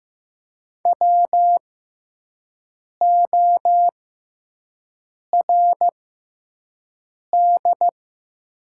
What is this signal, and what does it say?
Morse code, heard as a keyed tone, about 15 wpm, spaced as for 5 wpm, 700 Hz. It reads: WORD
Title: 6x Character Spacing - Top 100 Words - 15wpm